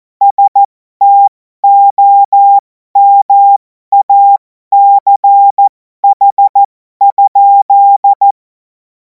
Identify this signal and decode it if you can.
Morse code, heard as a keyed tone, about 14 wpm, spaced as for 12 wpm, 800 Hz. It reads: STOMACH?
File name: Morsekurs / Witze